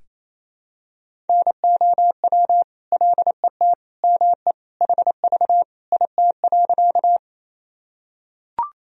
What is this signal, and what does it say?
Morse code, heard as a keyed tone, about 27 wpm, 700 Hz. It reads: NOW LET ME HV IT.
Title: Sentences from Top 300 Words and Abbreviations 28wpm